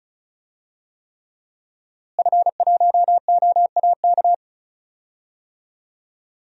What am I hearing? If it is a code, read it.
Morse code, heard as a keyed tone, about 36 wpm, 700 Hz. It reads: F1OAK
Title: DXCC-100 Call Signs - 5 Characters - Repeated 3x - 35wpm